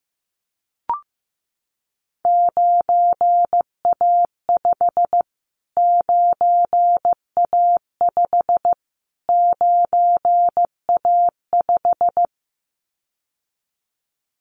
Morse code, heard as a keyed tone, about 15 wpm, 700 Hz. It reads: 9A5 9A5 9A5
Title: Callsign Prefixes for Top 25 DXCC - Repeated 3x - 15wpm